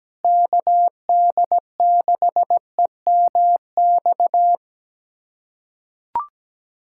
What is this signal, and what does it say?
Morse code, heard as a keyed tone, about 17 wpm, 700 Hz. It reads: KD6EMX
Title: USA Callsign DE Callsign 17wpm